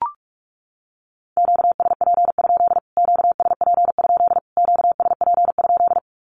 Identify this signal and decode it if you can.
Morse code, heard as a keyed tone, about 44 wpm, 700 Hz. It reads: QSP? QSP? QSP?
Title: Q-Codes Repeated 3x 45wpm